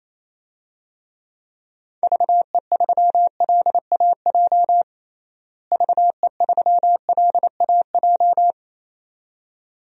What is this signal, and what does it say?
Morse code, heard as a keyed tone, about 28 wpm, 700 Hz. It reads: VE3LAJ VE3LAJ
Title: DXCC-100 Call Signs - 6 Characters - Repeated 2x - 28wpm